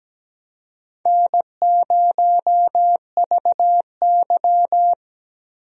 Morse code, heard as a keyed tone, about 17 wpm, 700 Hz. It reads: N0VY
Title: USA Call Signs - 4 Characters - Repeated 2x - 17wpm